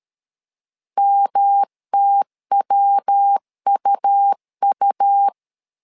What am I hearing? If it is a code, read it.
Morse code, heard as a keyed tone, about 13 wpm, 800 Hz. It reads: MTWUU
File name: LEZIONI / 3 MWRO / 3 MW 12wpm 1633